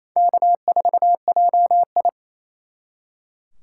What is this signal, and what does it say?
Morse code, heard as a keyed tone, about 34 wpm, 700 Hz. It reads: K4JI
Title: Speed Racing - USA 1x2 Call Signs 28wpm